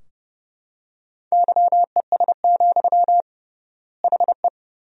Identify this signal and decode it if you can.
Morse code, heard as a keyed tone, about 32 wpm, 700 Hz. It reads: YES, HE
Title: 3x Word-Spacing - Sentences from Top 500 Words - 30wpm